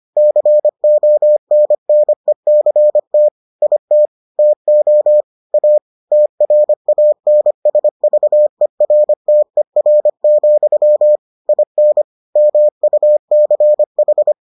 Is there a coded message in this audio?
Yes: Morse code from 0.2 to 14.3 s, heard as a keyed tone, about 25 wpm, 600 Hz. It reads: CONNECT IT TO A TRANSVERTER, IN MUCH